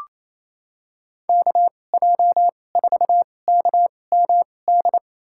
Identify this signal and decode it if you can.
Morse code, heard as a keyed tone, about 28 wpm, spaced as for 20 wpm, 700 Hz. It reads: KJ4KMD
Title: USA Call Signs - 6 Characters - 2x Character Spacing - 28wpm